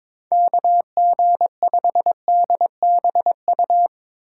Morse code, heard as a keyed tone, about 22 wpm, 700 Hz. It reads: KG5DBU
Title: USA Call Signs - 6 Characters - 3x Character Spacing - 22wpm